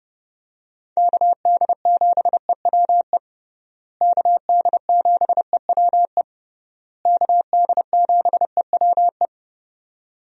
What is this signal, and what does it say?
Morse code, heard as a keyed tone, about 30 wpm, 700 Hz. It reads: KD7EWE KD7EWE KD7EWE
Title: USA Call Signs - 6 Characters - Repeated 3x - 30wpm